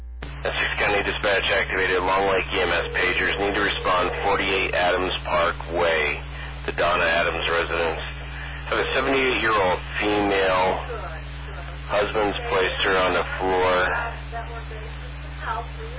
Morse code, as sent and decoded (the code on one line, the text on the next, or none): none